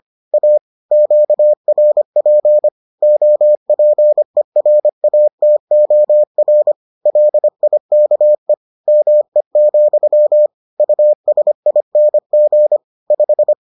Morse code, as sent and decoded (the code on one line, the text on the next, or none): .- / --.- .-. .--. / --- .--. .-. .- - --- .-. / .-.. .. -.- . / -- . --..-- / ..- ... .. -. --. / .....
A QRP OPERATOR LIKE ME, USING 5